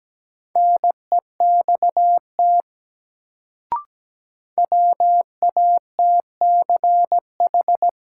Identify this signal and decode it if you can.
Morse code, heard as a keyed tone, about 17 wpm, 700 Hz. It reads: NEXT WATCH
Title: Sentences from Top 300 Words - New Word Drill - 17wpm